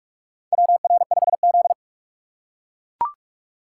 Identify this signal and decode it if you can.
Morse code, heard as a keyed tone, about 41 wpm, 700 Hz. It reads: WR5Z
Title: USA Call Signs - 4 Characters - 2x Character Spacing - 45wpm